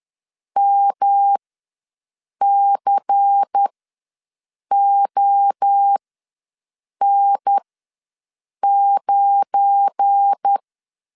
Morse code, heard as a keyed tone, about 10 wpm, spaced as for 6 wpm, 800 Hz. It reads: MCON9
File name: LEZIONI / 8 YJ45 / 8 YJ ws 10wpm 4043